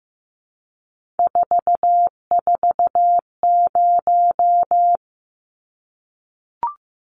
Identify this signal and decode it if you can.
Morse code, heard as a keyed tone, about 15 wpm, 700 Hz. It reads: 440
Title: Numbers - 3-Digit 15wpm